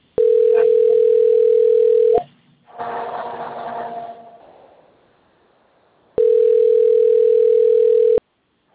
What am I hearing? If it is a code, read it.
Ringback tone (North America), 440 + 480 Hz.